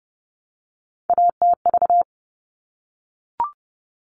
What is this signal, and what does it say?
Morse code, heard as a keyed tone, about 30 wpm, 700 Hz. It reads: ATV
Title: All Cut Numbers - 3-Digit 30wpm